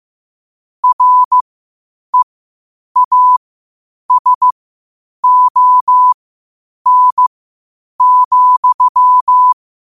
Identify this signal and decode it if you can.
Morse code, heard as a keyed tone, about 15 wpm, spaced as for 8 wpm, 1000 Hz. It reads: REASON,
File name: qotdfiles / 8wpm